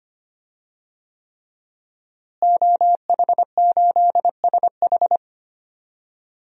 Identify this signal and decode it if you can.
Morse code, heard as a keyed tone, about 25 wpm, 700 Hz. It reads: OH8SH
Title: DXCC-100 Call Signs - 5 Characters - Repeated 3x - 25wpm